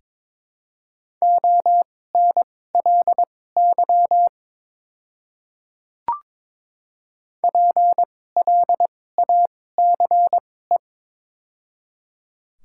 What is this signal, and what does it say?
Morse code, heard as a keyed tone, about 22 wpm, spaced as for 16 wpm, 700 Hz. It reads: ONLY PLACE
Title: 2x Character Spacing - Top 200 Words - 22wpm